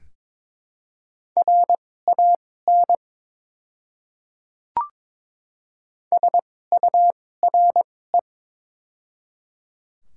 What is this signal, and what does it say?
Morse code, heard as a keyed tone, about 22 wpm, spaced as for 16 wpm, 700 Hz. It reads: RAN SURE
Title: 2x Character Spacing - Top 500 Words - 22wpm